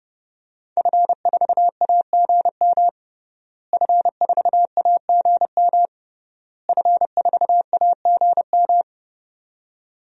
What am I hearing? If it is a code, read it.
Morse code, heard as a keyed tone, about 30 wpm, 700 Hz. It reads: F4AGM F4AGM F4AGM
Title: DXCC-100 Call Signs - 5 Characters - Repeated 3x - 30wpm